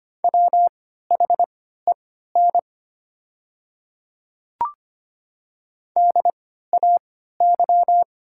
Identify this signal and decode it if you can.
Morse code, heard as a keyed tone, about 25 wpm, spaced as for 14 wpm, 700 Hz. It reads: WHEN DAY
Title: 3x Character Spacing - Top 200 Words - 25wpm